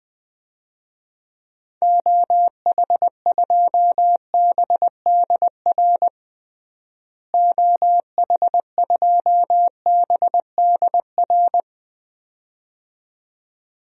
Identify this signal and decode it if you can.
Morse code, heard as a keyed tone, about 20 wpm, 700 Hz. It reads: OH2BDR OH2BDR
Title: DXCC-100 Call Signs - 6 Characters - Repeated 2x - 20wpm